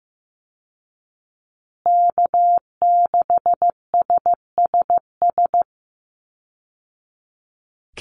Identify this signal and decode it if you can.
Morse code, heard as a keyed tone, about 15 wpm, 700 Hz. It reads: K6SSS
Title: Call Signs Repeated 3x - 15wpm